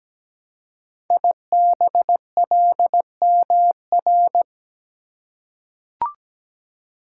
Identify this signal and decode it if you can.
Morse code, heard as a keyed tone, about 17 wpm, 700 Hz. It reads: IBLMR
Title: Code Groups - 5 Letters - 17wpm